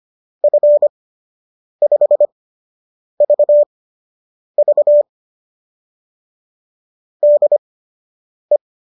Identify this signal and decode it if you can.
Morse code, heard as a keyed tone, about 25 wpm, spaced as for 8 wpm, 590 Hz. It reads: F5VV DE